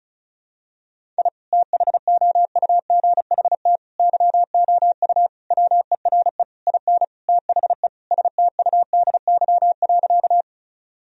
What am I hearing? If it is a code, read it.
Morse code, heard as a keyed tone, about 34 wpm, 700 Hz. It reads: I THOUGHT YOU WERE IN THE STUDY.
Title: Sentences from Top 1000 Words - Review Part 1 - 35wpm